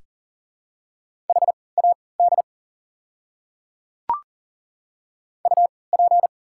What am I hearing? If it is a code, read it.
Morse code, heard as a keyed tone, about 40 wpm, spaced as for 23 wpm, 700 Hz. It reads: HAD UP
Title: 3x Character Spacing - Top 300 Words - 40wpm